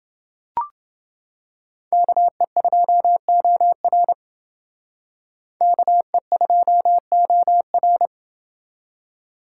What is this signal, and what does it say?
Morse code, heard as a keyed tone, about 28 wpm, 700 Hz. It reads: KE2OR KE2OR